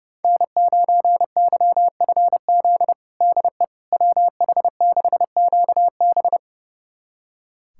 Morse code, heard as a keyed tone, about 30 wpm, 700 Hz. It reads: N9YFZ DE WH6QB
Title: USA Callsign DE Callsign - 5 Characters - 30wpm